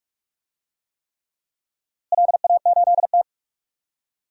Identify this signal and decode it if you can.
Morse code, heard as a keyed tone, about 46 wpm, 700 Hz. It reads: LA8T
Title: DXCC-100 Call Signs - 4 Characters - Repeated 3x - 45wpm